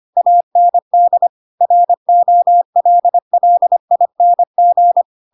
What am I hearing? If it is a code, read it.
Morse code, heard as a keyed tone, about 25 wpm, 700 Hz. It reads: AND ROLLING
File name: Pinocchio / Pinocchio ch7 25 25 ik1hge